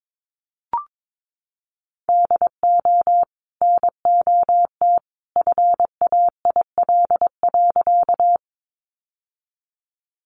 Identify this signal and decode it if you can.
Morse code, heard as a keyed tone, about 22 wpm, 700 Hz. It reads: DO NOT FAIL.
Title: Sentences from Top 1100 Words 22wpm